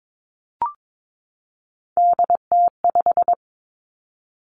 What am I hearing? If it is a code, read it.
Morse code, heard as a keyed tone, about 22 wpm, 700 Hz. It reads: DT5
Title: Three Challenging Letter-Number Combinations 22wpm